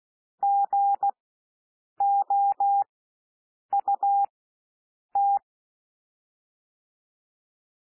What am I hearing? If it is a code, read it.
Morse code, heard as a keyed tone, about 16 wpm, spaced as for 7 wpm, 800 Hz. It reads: GOUT
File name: SOUND WORDS 16 wpm